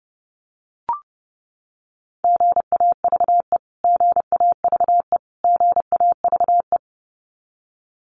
Morse code, heard as a keyed tone, about 30 wpm, 700 Hz. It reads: GAVE GAVE GAVE
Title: Top 500 Words Repeated 30wpm